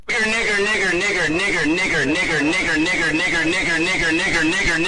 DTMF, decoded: none